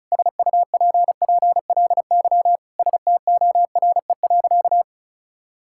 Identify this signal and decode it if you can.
Morse code, heard as a keyed tone, about 36 wpm, 700 Hz. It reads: SUPPLY STORE.
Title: Sentences from Top 1000 Words - Review Part 5 - 35wpm